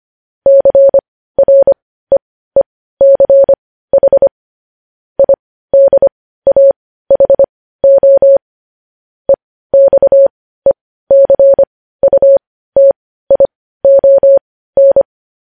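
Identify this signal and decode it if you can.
Morse code, heard as a keyed tone, about 25 wpm, spaced as for 15 wpm, 550 Hz. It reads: CREECH IDAHO EXECUTION
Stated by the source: Morse code news headlines sent at 15, preceded by vvv vvv.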